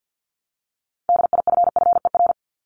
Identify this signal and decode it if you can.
Morse code, heard as a keyed tone, about 50 wpm, 700 Hz. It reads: DIFFER
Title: Top 500 Words 50wpm